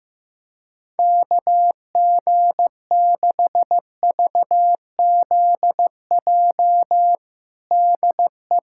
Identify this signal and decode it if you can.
Morse code, heard as a keyed tone, about 16 wpm, 700 Hz. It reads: KG6VZJ DE
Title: USA Callsign DE Callsign - 6 Characters - 15wpm